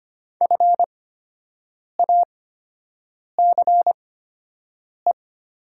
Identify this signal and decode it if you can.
Morse code, heard as a keyed tone, about 25 wpm, spaced as for 7 wpm, 700 Hz. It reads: FACE
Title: 8x Character Spacing - Top 500 Words - 25wpm